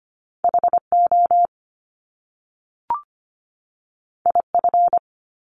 Morse code, heard as a keyed tone, about 25 wpm, 700 Hz. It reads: HO IF
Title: Top Two Letter Combinations in Words 25wpm